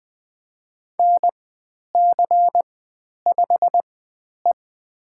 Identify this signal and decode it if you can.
Morse code, heard as a keyed tone, about 20 wpm, spaced as for 10 wpm, 700 Hz. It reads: NC5E
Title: 109 Lesson 18g - Letter m - Call Signs 20wpm